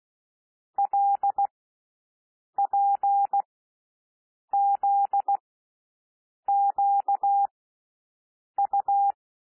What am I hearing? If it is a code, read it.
Morse code, heard as a keyed tone, about 16 wpm, spaced as for 6 wpm, 800 Hz. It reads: LPZQU
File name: SOUND LETTERS 16 wpm